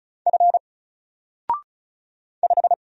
Morse code, heard as a keyed tone, about 36 wpm, 700 Hz. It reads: F 5
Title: Single Character (Letter:Number:Punctuation:Prosign) 35wpm